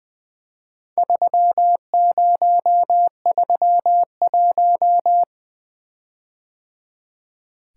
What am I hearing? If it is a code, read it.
Morse code, heard as a keyed tone, about 20 wpm, 700 Hz. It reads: 3031
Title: Four Letter-Number Combinations - 20wpm